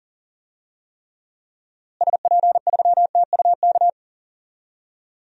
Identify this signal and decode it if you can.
Morse code, heard as a keyed tone, about 40 wpm, 700 Hz. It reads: SP3TUK
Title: DXCC-100 Call Signs - 6 Characters - Repeated 3x - 40wpm